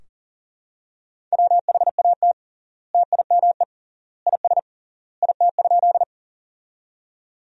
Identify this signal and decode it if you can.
Morse code, heard as a keyed tone, about 40 wpm, 700 Hz. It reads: WHAT TIME IS IT?
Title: 3x Word-Spacing - Sentences from Top 100 Words - 40wpm